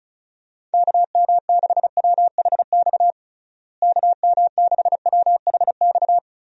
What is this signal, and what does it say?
Morse code, heard as a keyed tone, about 34 wpm, 700 Hz. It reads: KM6WHX KM6WHX
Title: USA Call Signs - 6 Characters - Repeated 3x - 35wpm